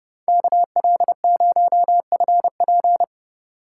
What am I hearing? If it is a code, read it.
Morse code, heard as a keyed tone, about 32 wpm, 700 Hz. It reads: KL0FP